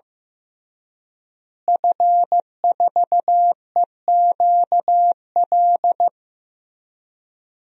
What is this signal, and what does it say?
Morse code, heard as a keyed tone, about 15 wpm, 700 Hz. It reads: F4EQL
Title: DXCC-100 Call Signs - 5 Characters - Repeated 3x - 15wpm